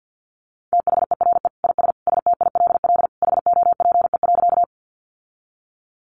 Morse code, heard as a keyed tone, about 50 wpm, 700 Hz. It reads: THERE IS STILL HOPE.
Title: Sentences from Top 1000 Words 50wpm